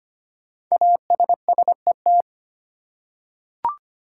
Morse code, heard as a keyed tone, about 25 wpm, 700 Hz. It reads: ASSET
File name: 5 Letter Words - First 3 Letters Given - 25wpm